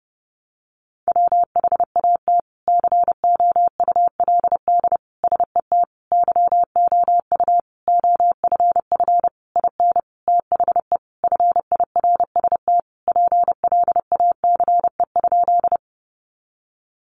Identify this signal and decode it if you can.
Morse code, heard as a keyed tone, about 30 wpm, 700 Hz. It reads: WHAT COULD SET YOU OFF IN THE FIRST PLACE?
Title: Sentences from Top 200 Words 30wpm